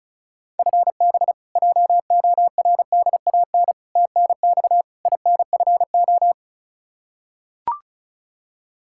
Morse code, heard as a keyed tone, about 36 wpm, 700 Hz. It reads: FB JORDAN TNX INFO
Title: Practice QSO - Exchange 3 - More Info - 35wpm